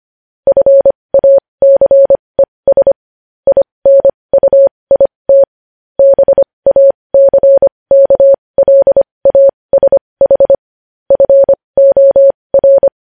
Morse code, heard as a keyed tone, about 25 wpm, spaced as for 20 wpm, 550 Hz. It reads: FACES INUIT BACKLASH FOR
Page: news headlines in morse code at 20 wpm